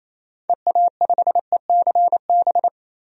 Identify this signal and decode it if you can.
Morse code, heard as a keyed tone, about 28 wpm, 700 Hz. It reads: EA5ECB
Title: DXCC-100 Call Signs - 6 Characters - 28wpm